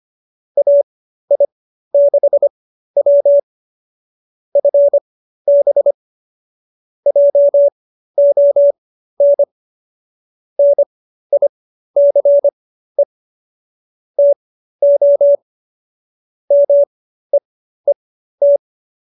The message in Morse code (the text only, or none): AI6W FB JON NICE TO MEET